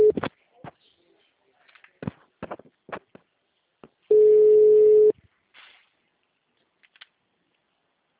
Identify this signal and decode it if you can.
Ringback tone (Europe, 425 Hz).